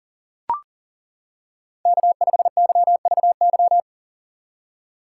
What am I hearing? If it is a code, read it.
Morse code, heard as a keyed tone, about 37 wpm, 700 Hz. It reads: K5YVY
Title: K1USN SST QSO Practice - 40wpm